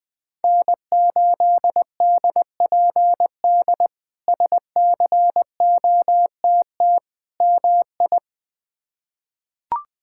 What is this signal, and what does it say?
Morse code, heard as a keyed tone, about 20 wpm, 700 Hz. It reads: N8DPD SCOTT MI
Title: K1USN SST QSO Practice - 20wpm